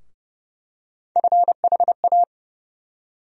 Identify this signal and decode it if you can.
Morse code, heard as a keyed tone, about 30 wpm, 700 Hz. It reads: FHA
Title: USA Call Sign Postfixes - 3 Letters 30wpm